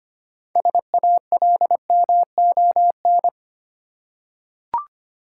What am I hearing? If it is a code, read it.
Morse code, heard as a keyed tone, about 25 wpm, 690 Hz. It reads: SALMON
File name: practice / 25wpm / Easy Words with Hints - 25wpm